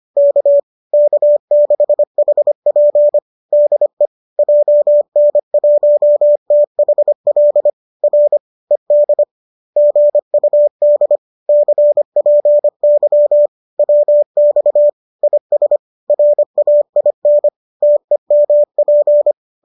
Morse code, heard as a keyed tone, about 25 wpm, 590 Hz. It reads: K K6HP DE JN1THL R ED GUD CPY WX IS RAIN TEMP
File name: qso205 25